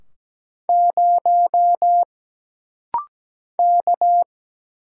Morse code, heard as a keyed tone, about 17 wpm, 700 Hz. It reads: 0 K
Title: Single Letter-Number Alt - Rapid-Fire 17wpm